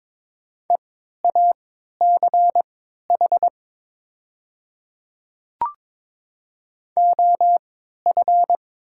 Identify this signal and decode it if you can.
Morse code, heard as a keyed tone, about 22 wpm, spaced as for 12 wpm, 700 Hz. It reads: EACH OF
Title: 3x Character Spacing - Top 100 Words - 22wpm